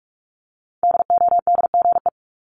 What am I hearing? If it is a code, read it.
Morse code, heard as a keyed tone, about 46 wpm, 700 Hz. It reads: DODGE
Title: Multi-letter Phonemes Repeated 3x 45wpm